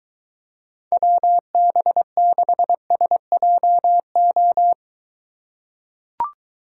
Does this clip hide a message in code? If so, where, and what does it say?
Yes: Morse code from 0.9 to 4.7 s, heard as a keyed tone, about 22 wpm, 700 Hz. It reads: WB6SJO